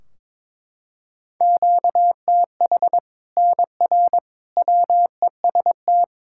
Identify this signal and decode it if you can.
Morse code, heard as a keyed tone, about 22 wpm, 700 Hz. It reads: QTH NR WEST
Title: Practice QSO - Exchange 1 - Beginning - 22wpm